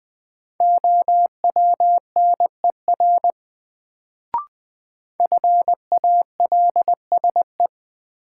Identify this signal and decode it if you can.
Morse code, heard as a keyed tone, about 21 wpm, 700 Hz. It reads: OWNER FALSE
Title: Sentences from Top 1600 Words - New Word Drill Alt - 20wpm